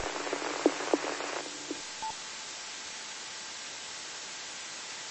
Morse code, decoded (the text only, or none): none